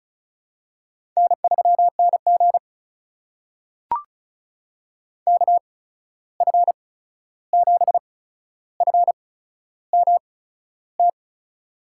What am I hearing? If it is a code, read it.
Morse code, heard as a keyed tone, about 36 wpm, spaced as for 13 wpm, 700 Hz. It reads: N3NG KF7FMT
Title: USA Call Signs - 8x Character Spacing - 35wpm